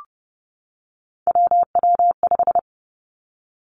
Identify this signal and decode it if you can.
Morse code, heard as a keyed tone, about 30 wpm, 700 Hz. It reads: WW5
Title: Three Challenging Letter-Number Combinations 30wpm